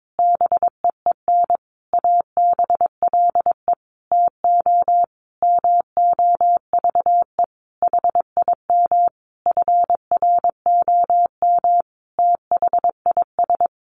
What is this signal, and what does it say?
Morse code, heard as a keyed tone, about 22 wpm, 700 Hz. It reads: BEEN ABLE TO MOVE HIM FROM THIS
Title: Sentences from Top 1300 Words 22wpm